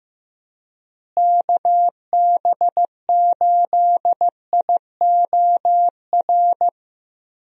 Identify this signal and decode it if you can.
Morse code, heard as a keyed tone, about 16 wpm, 700 Hz. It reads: KB8IOR
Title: USA Call Signs - 6 Characters - Repeated 3x - 15wpm